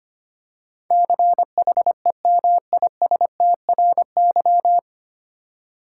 Morse code, heard as a keyed tone, about 25 wpm, 700 Hz. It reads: CHEMISTRY
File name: Nine Letter Words - 25wpm